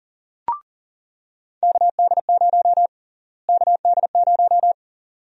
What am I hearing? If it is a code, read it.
Morse code, heard as a keyed tone, about 40 wpm, 700 Hz. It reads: KD0 KD0
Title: USA Call Sign Prefixes - Repeated 2x - 40wpm